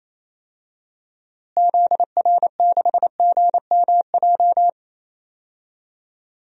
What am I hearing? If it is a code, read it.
Morse code, heard as a keyed tone, about 28 wpm, 700 Hz. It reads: ZR6GMJ
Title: DXCC-100 Call Signs - 6 Characters - Repeated 3x - 28wpm